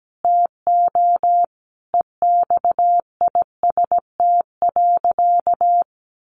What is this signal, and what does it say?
Morse code, heard as a keyed tone, about 17 wpm, 700 Hz. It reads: TO EXIST.
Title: Sentences from Top 1400 Words 17wpm